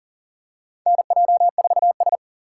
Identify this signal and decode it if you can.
Morse code, heard as a keyed tone, about 40 wpm, 700 Hz. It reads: NJ4S
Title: USA Call Signs - 4 Characters - 4x Character Spacing - 40wpm